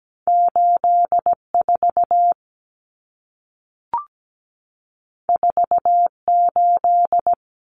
Morse code, heard as a keyed tone, about 17 wpm, 700 Hz. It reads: 84 48
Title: Common Cut Numbers - 2-Digit 17wpm